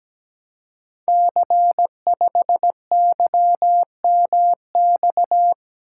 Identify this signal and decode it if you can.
Morse code, heard as a keyed tone, about 17 wpm, 700 Hz. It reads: C5YMX
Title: Code Groups - 5 Letter:Numbers - 17wpm